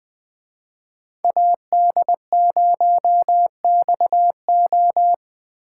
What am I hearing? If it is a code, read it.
Morse code, heard as a keyed tone, about 21 wpm, 700 Hz. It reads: AD0XO